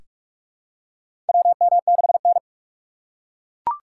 Morse code, heard as a keyed tone, about 42 wpm, 700 Hz. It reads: WM6N
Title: USA Call Signs - 4 Characters - 8x Character Spacing - 45wpm